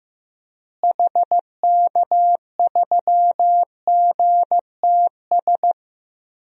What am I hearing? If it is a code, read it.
Morse code, heard as a keyed tone, about 15 wpm, 700 Hz. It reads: HK3GTS
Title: DXCC-100 Call Signs - 6 Characters - Repeated 3x - 15wpm